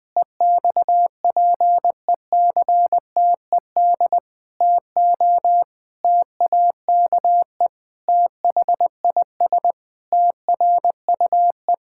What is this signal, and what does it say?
Morse code, heard as a keyed tone, about 20 wpm, 700 Hz. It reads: EXPECTED TO TAKE THIS TRUE
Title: Pangram Challenge v2 - 20wpm